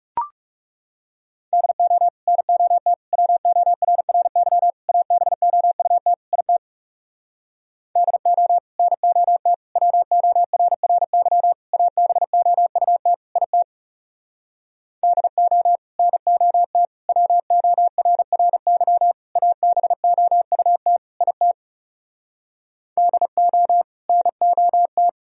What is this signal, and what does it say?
Morse code, heard as a keyed tone, about 37 wpm, 700 Hz. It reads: DO NOT WORRY ABOUT IT DO NOT WORRY ABOUT IT DO NOT WORRY ABOUT IT DO NOT
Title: Speed-Racing - Sets of 5 Words 30wpm